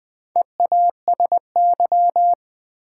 Morse code, heard as a keyed tone, about 20 wpm, 700 Hz. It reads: EASY
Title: Sentences from Top 700 Words - New Word Drill - 20wpm